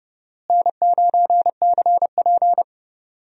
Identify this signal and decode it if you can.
Morse code, heard as a keyed tone, about 31 wpm, 700 Hz. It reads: N9CP
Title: USA Call Signs - 4 Characters - Repeated 2x - 30wpm